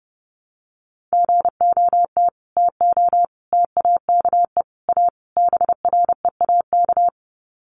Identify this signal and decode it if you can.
Morse code, heard as a keyed tone, about 30 wpm, 700 Hz. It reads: GOT TO TAKE A BREAK
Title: Sets of 5 Words Encore 30wpm